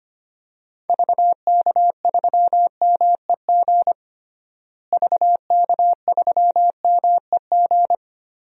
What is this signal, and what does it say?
Morse code, heard as a keyed tone, about 25 wpm, 700 Hz. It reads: VK3MEG VK3MEG
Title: DXCC-100 Call Signs - 6 Characters - Repeated 3x - 25wpm